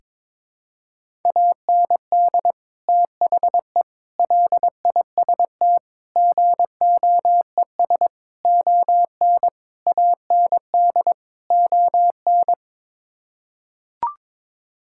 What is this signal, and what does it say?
Morse code, heard as a keyed tone, about 22 wpm, 700 Hz. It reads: AND THE LIST GOES ON AND ON
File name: practice / 22wpm / Sets of 7 Words - 22wpm